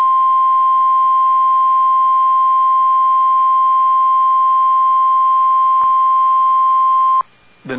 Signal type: Emergency Alert System (SAME) tones